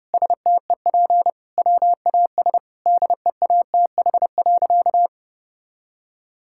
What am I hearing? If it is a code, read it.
Morse code, heard as a keyed tone, about 32 wpm, 700 Hz. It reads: STEP WAS DEATH.
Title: Sentences from Top 1600 Words 30wpm